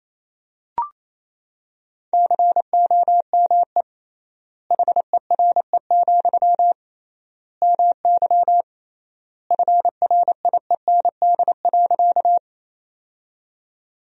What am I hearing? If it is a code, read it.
Morse code, heard as a keyed tone, about 28 wpm, 700 Hz. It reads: COME HERE, MY FRIEND.
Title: 3x Word-Spacing - Sentences from Top 300 Words - 28wpm